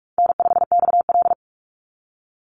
Morse code, heard as a keyed tone, about 44 wpm, 700 Hz. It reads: N5XL
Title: NA SOTA Chaser Callsigns 45wpm